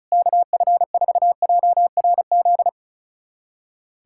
Morse code, heard as a keyed tone, about 34 wpm, 700 Hz. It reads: KF4JRZ
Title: USA Callsign DE Callsign 35wpm